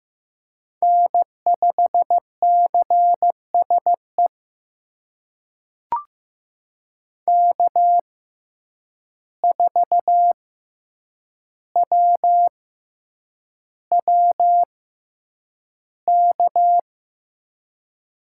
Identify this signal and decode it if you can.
Morse code, heard as a keyed tone, about 15 wpm, spaced as for 8 wpm, 700 Hz. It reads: N5CSE K4WWK
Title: USA Call Signs - 5 Characters - 6x Character Spacing - 15wpm